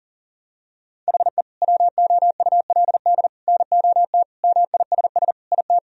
Morse code, heard as a keyed tone, about 40 wpm, 700 Hz. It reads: HE WOULD NOT MISS IT
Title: Sentences from Top 1000 Words - Review Part 3 - 40wpm